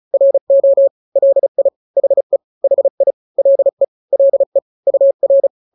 Morse code, heard as a keyed tone, about 35 wpm, 550 Hz. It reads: RO LI HE HI LE LE UR
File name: n-grams / 2-grams 35 30 WPM